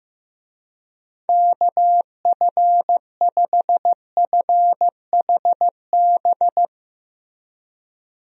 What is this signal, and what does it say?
Morse code, heard as a keyed tone, about 15 wpm, 700 Hz. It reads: KF5FHB
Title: USA Call Signs - 6 Characters - Repeated 3x - 15wpm